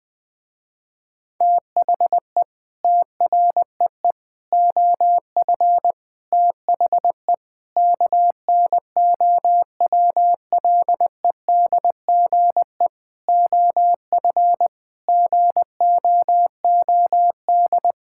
Morse code, heard as a keyed tone, about 20 wpm, 700 Hz. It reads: THE TREE OF THE KNOWLEDGE OF GOOD
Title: Sentences with 9-Word Phrases - Spoken Beginning - 20wpm